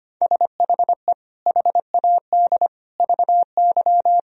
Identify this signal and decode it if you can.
Morse code, heard as a keyed tone, about 25 wpm, 700 Hz. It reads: SHE HAD VY
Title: Sentences from Top 400 Words and Abbreviations 25wpm